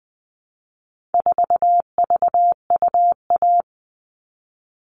Morse code, heard as a keyed tone, about 20 wpm, 700 Hz. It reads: 4VUA
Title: All Cut Numbers - 4-Digit 20wpm